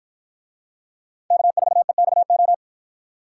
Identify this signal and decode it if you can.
Morse code, heard as a keyed tone, about 50 wpm, 700 Hz. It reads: K4EXO